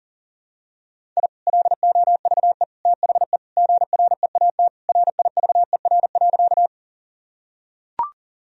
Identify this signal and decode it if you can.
Morse code, heard as a keyed tone, about 40 wpm, 700 Hz. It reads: I LOVE THE GREAT RIVER.
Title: Sentences from Top 1000 Words - Review Part 1 - 40wpm